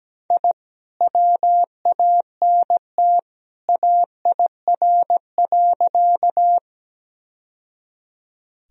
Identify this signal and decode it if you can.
Morse code, heard as a keyed tone, about 17 wpm, 700 Hz. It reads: I WANT AIR.
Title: Sentences from Top 1000 Words - Review Part 1 - 17wpm